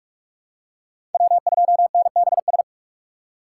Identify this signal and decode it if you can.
Morse code, heard as a keyed tone, about 46 wpm, 700 Hz. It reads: W2NBS
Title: USA Call Signs - 5 Characters - 2x Character Spacing - 45wpm